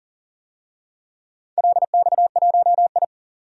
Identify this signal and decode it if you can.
Morse code, heard as a keyed tone, about 39 wpm, 700 Hz. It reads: LX1I
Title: DXCC-100 Call Signs - 4 Characters - 40wpm